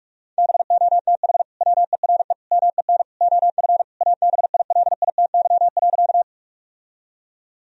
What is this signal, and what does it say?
Morse code, heard as a keyed tone, about 46 wpm, 700 Hz. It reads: BOTH WERE MEN OF ABILITY.
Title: Sentences from Top 1000 Words - Review Part 3 - 45wpm